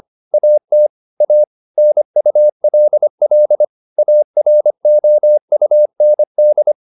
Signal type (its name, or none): Morse code